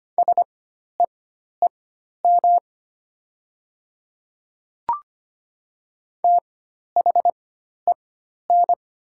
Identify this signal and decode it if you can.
Morse code, heard as a keyed tone, about 25 wpm, spaced as for 12 wpm, 690 Hz. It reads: SEEM THEN
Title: 4x Character Spacing - Top 500 Words - 25wpm